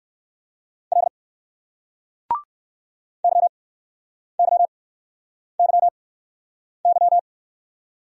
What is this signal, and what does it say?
Morse code, heard as a keyed tone, about 60 wpm, spaced as for 9 wpm, 700 Hz. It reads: L YYYY